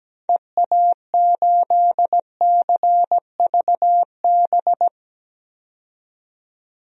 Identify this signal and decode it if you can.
Morse code, heard as a keyed tone, about 17 wpm, 700 Hz. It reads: EA8CVB
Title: DXCC-100 Call Signs - 6 Characters - 17wpm